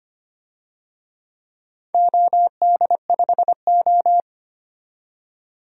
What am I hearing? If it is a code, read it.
Morse code, heard as a keyed tone, about 25 wpm, 700 Hz. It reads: OD5O
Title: DXCC-100 Call Signs - 4 Characters - 25wpm